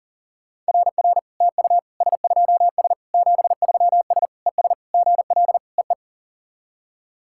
Morse code, heard as a keyed tone, about 43 wpm, 700 Hz. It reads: RR TU S2S 73S ES GL EE
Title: SOTA QSO Practice 40wpm